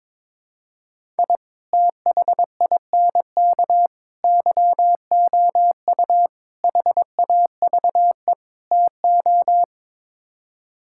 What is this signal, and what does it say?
Morse code, heard as a keyed tone, about 21 wpm, 700 Hz. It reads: I THINK YOU HAVE TO
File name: practice / 22wpm / Build A Phrase - 5-Word Phrases - 22wpm